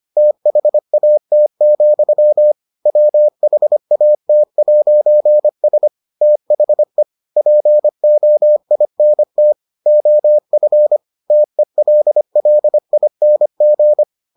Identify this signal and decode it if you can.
Morse code, heard as a keyed tone, about 25 wpm, 600 Hz. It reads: THAT, WHAT'S THE POINT OF TELLING